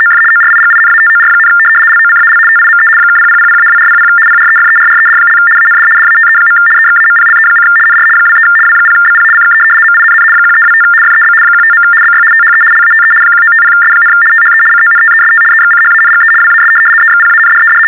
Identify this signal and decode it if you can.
RTTY, about 75 baud. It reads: OF ORIGINATOR, LOCAL TIME (0000-2400 PREFERRED) AND DATE. EXAMPLE: NR 23 W0BP CK MINN 1325 FEB 15. BY THE LATE 1950S, THE CONTEST EXCHANGE WAS EXPANDED
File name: teletype-RTTY-75